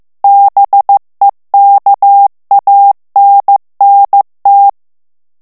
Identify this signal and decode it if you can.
Morse code, heard as a keyed tone, about 15 wpm, 800 Hz. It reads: BEKANNT
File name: morse mp3 / ALLEMAND texte clair 1 vitesse 14 mots minute